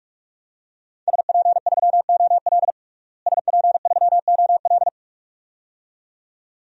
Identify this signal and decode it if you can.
Morse code, heard as a keyed tone, about 46 wpm, 700 Hz. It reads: SP3OL SP3OL
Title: DXCC-100 Call Signs - 5 Characters - Repeated 2x - 45wpm